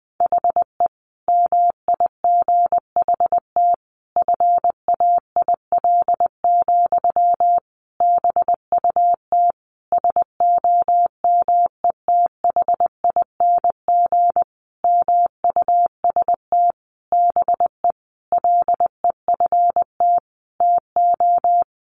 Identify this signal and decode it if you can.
Morse code, heard as a keyed tone, about 20 wpm, 700 Hz. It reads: HE MIGHT FAIL, BUT SOMETHING MUST BE LEFT TO
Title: Sentences from Top 1100 Words 20wpm